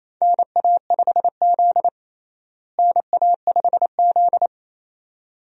Morse code, heard as a keyed tone, about 28 wpm, 700 Hz. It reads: NA5Z NA5Z
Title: USA Call Signs - 4 Characters - Repeated 2x - 28wpm